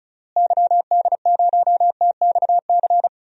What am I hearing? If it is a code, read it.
Morse code, heard as a keyed tone, about 34 wpm, 700 Hz. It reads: YD0TXC
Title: DXCC-100 Call Signs - 6 Characters - Repeated 3x - 35wpm